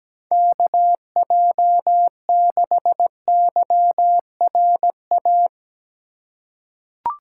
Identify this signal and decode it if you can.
Morse code, heard as a keyed tone, about 17 wpm, 700 Hz. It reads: KJ6YRA
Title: USA Call Signs - 6 Characters - 3x Character Spacing - 17wpm